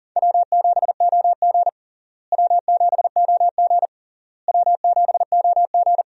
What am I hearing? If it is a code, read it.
Morse code, heard as a keyed tone, about 37 wpm, 700 Hz. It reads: W7OG W7OG W7OG
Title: USA Call Signs - 4 Characters - Repeated 3x - 40wpm